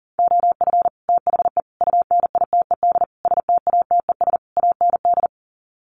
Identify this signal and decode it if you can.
Morse code, heard as a keyed tone, about 40 wpm, 700 Hz. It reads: OF THE UNITED STATES AND
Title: Sets of 5 Words 40wpm